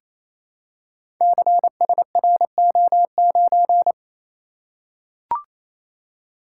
Morse code, heard as a keyed tone, about 28 wpm, 700 Hz. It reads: CSRO9
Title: Code Groups - 5 Letter:Numbers - 28wpm